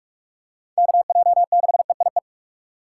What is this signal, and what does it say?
Morse code, heard as a keyed tone, about 41 wpm, 700 Hz. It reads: KJ6EIE
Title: USA Call Signs - 6x Character Spacing - 45wpm